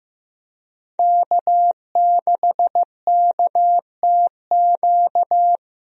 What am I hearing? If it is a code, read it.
Morse code, heard as a keyed tone, about 15 wpm, 700 Hz. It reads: K6KTQ
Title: USA Call Signs - 5 Characters - Repeated 3x - 15wpm